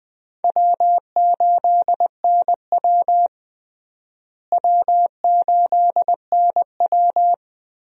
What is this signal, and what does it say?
Morse code, heard as a keyed tone, about 20 wpm, 700 Hz. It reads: W8NW W8NW
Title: USA Call Signs - 4 Characters - Repeated 3x - 20wpm